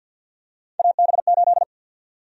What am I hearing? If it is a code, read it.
Morse code, heard as a keyed tone, about 52 wpm, 700 Hz. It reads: AB8
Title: USA Call Sign Prefixes - 50wpm